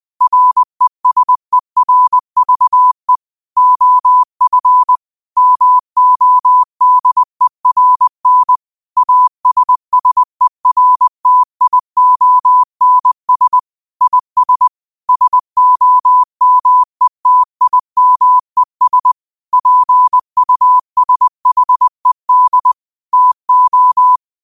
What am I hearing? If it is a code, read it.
Morse code, heard as a keyed tone, about 20 wpm, 1000 Hz. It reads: RESERVE OF MODERN ASSERTIONS IS SOMETIMES PUSHED TO
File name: qotdfiles / 20wpm